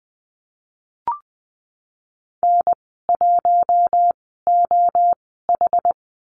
Morse code, heard as a keyed tone, about 19 wpm, spaced as for 14 wpm, 700 Hz. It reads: N1OH
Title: 193 Lesson 31g - Letter v - Call Signs 20wpm